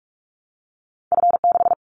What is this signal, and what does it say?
Morse code, heard as a keyed tone, about 44 wpm, 700 Hz. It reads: F6
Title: Callsign Prefixes for Top 50 DXCC - 45wpm